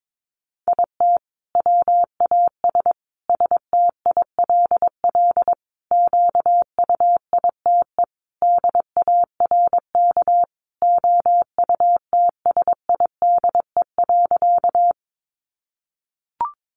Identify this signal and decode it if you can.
Morse code, heard as a keyed tone, about 22 wpm, 700 Hz. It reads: IT WAS STILL QUITE DARK OUTSIDE.
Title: Sentences from Top 900 Words 22wpm